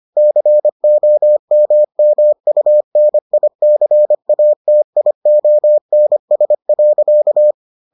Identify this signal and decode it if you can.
Morse code, heard as a keyed tone, about 25 wpm, 600 Hz. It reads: COMMUNICATIONS.